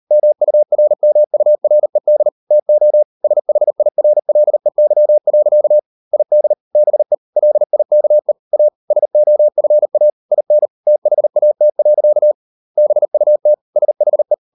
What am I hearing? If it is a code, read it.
Morse code, heard as a keyed tone, about 40 wpm, 600 Hz. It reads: MURMURED TO SHIRLEY. ID BE LIKE A SOFA IN THAT. BUT SHE